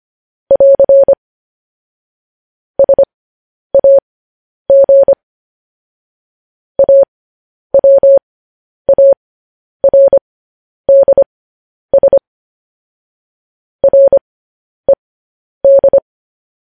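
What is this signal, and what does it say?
Morse code, heard as a keyed tone, about 25 wpm, spaced as for 10 wpm, 550 Hz. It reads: <AR> SAG AWARDS RED
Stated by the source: News headlines sent in Morse code at 10 wpm.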